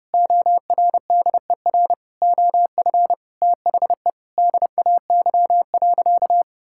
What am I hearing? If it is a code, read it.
Morse code, heard as a keyed tone, about 30 wpm, 700 Hz. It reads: ORDER OF THE DAY.
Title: Sentences from Top 1000 Words - Review Part 5 - 30wpm